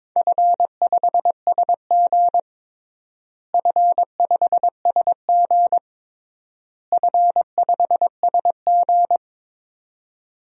Morse code, heard as a keyed tone, about 22 wpm, 700 Hz. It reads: F5SG F5SG F5SG
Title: DXCC-100 Call Signs - 4 Characters - Repeated 3x - 22wpm